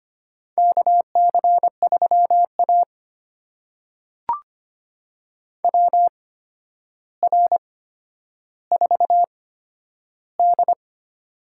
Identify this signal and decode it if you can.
Morse code, heard as a keyed tone, about 25 wpm, spaced as for 11 wpm, 700 Hz. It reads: KC3A WR4D
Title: USA Call Signs - 4 Characters - 8x Character Spacing - 25wpm